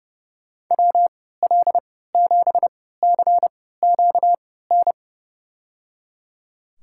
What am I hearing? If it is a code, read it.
Morse code, heard as a keyed tone, about 30 wpm, spaced as for 17 wpm, 700 Hz. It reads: WL7CQN
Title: USA Call Signs - 6 Characters - 3x Character Spacing - 30wpm